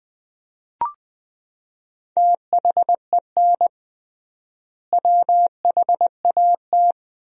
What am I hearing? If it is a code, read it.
Morse code, heard as a keyed tone, about 21 wpm, 700 Hz. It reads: THEN WHAT
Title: 3x Word-Spacing - Sentences from Top 400 Words - 20wpm